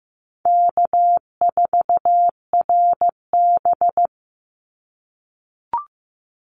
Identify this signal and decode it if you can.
Morse code, heard as a keyed tone, about 15 wpm, 700 Hz. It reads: K4RB
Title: Easy Call Signs - 15wpm